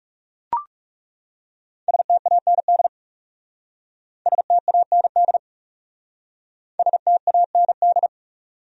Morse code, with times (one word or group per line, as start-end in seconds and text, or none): 1.9-2.9 STAND
4.3-5.4 STAND
6.8-8.1 STAND